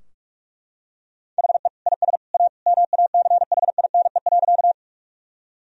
Morse code, heard as a keyed tone, about 42 wpm, 700 Hz. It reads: HE IS A MACHINE.
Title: Sentences from Top 1000 Words - Review Part 2 - 45wpm